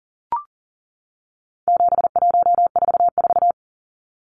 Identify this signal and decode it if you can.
Morse code, heard as a keyed tone, about 40 wpm, 700 Hz. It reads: 7144
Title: Numbers - 4-Digit 40wpm